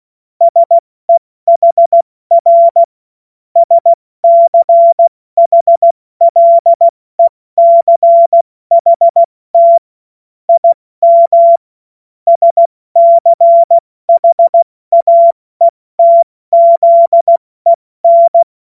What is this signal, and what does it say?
Morse code, heard as a keyed tone, about 16 wpm, spaced as for 14 wpm, 680 Hz. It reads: SEHR SCHLECHT IM SCHAETZEN
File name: Morsekurs / Witze / Witz 14. September 2021, 70 F80